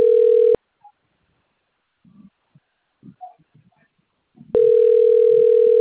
Ringback tone (North America), 440 + 480 Hz.